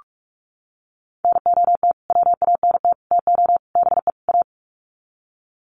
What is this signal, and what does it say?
Morse code, heard as a keyed tone, about 44 wpm, 700 Hz. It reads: NOT WANT TO BE A
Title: Sets of 5 Words Encore 45wpm